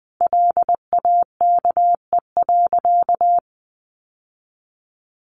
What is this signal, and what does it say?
Morse code, heard as a keyed tone, about 20 wpm, 700 Hz. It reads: LAKE.
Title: Sentences from Top 1100 Words 20wpm